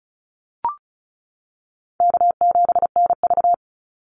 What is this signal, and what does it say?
Morse code, heard as a keyed tone, about 34 wpm, 700 Hz. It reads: K7NV
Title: Call Signs - 35wpm